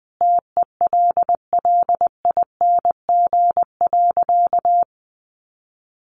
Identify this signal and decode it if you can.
Morse code, heard as a keyed tone, about 20 wpm, 700 Hz. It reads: TELLING.
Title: A Sound Of Thunder 20wpm